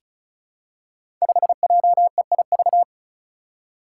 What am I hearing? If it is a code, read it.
Morse code, heard as a keyed tone, about 36 wpm, 700 Hz. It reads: 5JEIV
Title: Five Challenging Letter-Number Combinations 35wpm